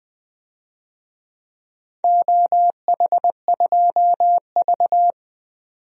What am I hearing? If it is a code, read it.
Morse code, heard as a keyed tone, about 20 wpm, 700 Hz. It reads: OH2V
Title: DXCC-100 Call Signs - 4 Characters - 20wpm